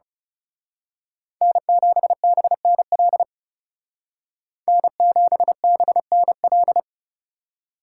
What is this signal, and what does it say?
Morse code, heard as a keyed tone, about 31 wpm, 700 Hz. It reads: N7BNL N7BNL
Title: Speed Racing - USA 1x3 Call Signs 30wpm